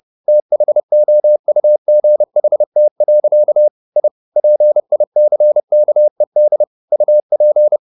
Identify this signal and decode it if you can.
Morse code, heard as a keyed tone, about 30 wpm, 600 Hz. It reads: THOUGHT. I PICKED UP
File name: the old man and the sea / the old man and the sea 1 30wpm